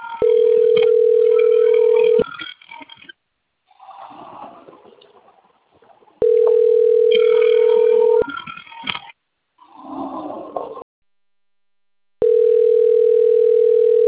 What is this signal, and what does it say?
Ringback tone (North America), 440 + 480 Hz.
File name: rg-1600-unknown- - - .2186